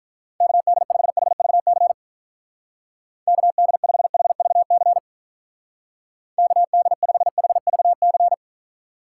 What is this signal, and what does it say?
Morse code, heard as a keyed tone, about 45 wpm, 700 Hz. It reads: KD5HVC KD5HVC KD5HVC